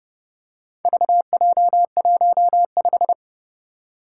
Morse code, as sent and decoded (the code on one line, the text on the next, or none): ...- .--- .---- .....
VJ15